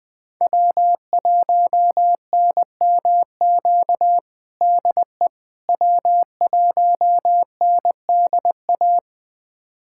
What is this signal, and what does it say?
Morse code, heard as a keyed tone, about 20 wpm, 700 Hz. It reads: W1NMQ DE W1NDA
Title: USA Callsign DE Callsign 20wpm